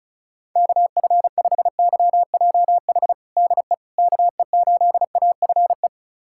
Morse code, heard as a keyed tone, about 36 wpm, 700 Hz. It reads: KF5YJH DE KE8AFE
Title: USA Callsign DE Callsign - 6 Characters - 35wpm